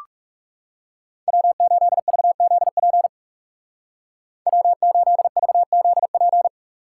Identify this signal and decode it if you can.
Morse code, heard as a keyed tone, about 40 wpm, 700 Hz. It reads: W8VZP W8VZP